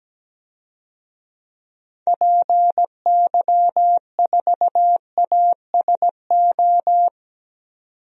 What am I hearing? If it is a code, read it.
Morse code, heard as a keyed tone, about 17 wpm, 700 Hz. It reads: PY4ASO
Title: DXCC-100 Call Signs - 6 Characters - 17wpm